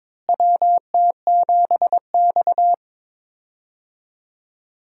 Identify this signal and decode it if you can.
Morse code, heard as a keyed tone, about 22 wpm, 700 Hz. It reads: WT7X
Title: USA Call Signs - 4 Characters - Repeated 3x - 22wpm